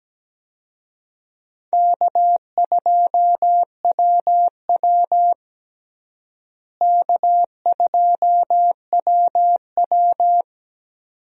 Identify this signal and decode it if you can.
Morse code, heard as a keyed tone, about 17 wpm, 700 Hz. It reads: K2WW K2WW
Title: USA Call Signs - 4 Characters - Repeated 2x - 17wpm